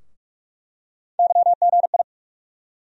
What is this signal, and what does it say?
Morse code, heard as a keyed tone, about 44 wpm, 700 Hz. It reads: YGI
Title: USA Call Sign Postfixes - 3 Letters 45wpm